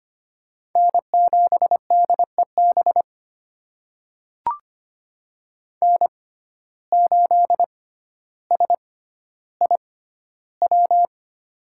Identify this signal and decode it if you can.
Morse code, heard as a keyed tone, about 25 wpm, spaced as for 13 wpm, 700 Hz. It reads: N7DEB N8SIW
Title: USA Call Signs - 5 Characters - 6x Character Spacing - 25wpm